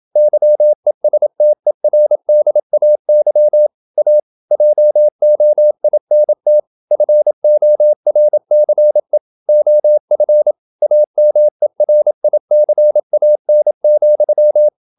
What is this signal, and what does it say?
Morse code, heard as a keyed tone, about 27 wpm, 600 Hz. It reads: YESTERDAY A JOINT FORCE OF AMERICAN,
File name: Never 0027wpm